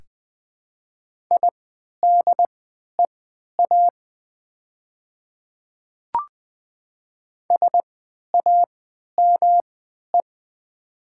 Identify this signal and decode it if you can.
Morse code, heard as a keyed tone, about 20 wpm, spaced as for 11 wpm, 700 Hz. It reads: IDEA SAME
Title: 3x Character Spacing - Top 300 Words - 20wpm